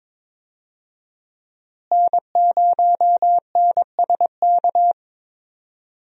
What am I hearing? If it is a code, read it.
Morse code, heard as a keyed tone, about 22 wpm, 700 Hz. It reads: N0NSK
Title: USA Call Signs - 5 Characters - Repeated 2x - 22wpm